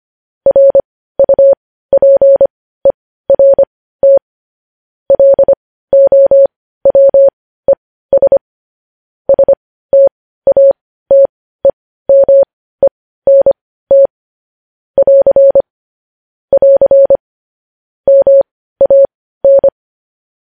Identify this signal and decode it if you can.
Morse code, heard as a keyed tone, about 25 wpm, spaced as for 15 wpm, 550 Hz. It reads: RUPERT LOWES STATEMENT <AR> <AR> MAN
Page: News headlines in Morse code at 15 wpm